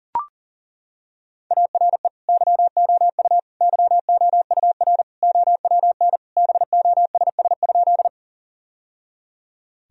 Morse code, heard as a keyed tone, about 37 wpm, 700 Hz. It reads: ARE YOU YOUR OWN BOSS?
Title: Sentences from Top 1000 Words - Review Part 4 - 40wpm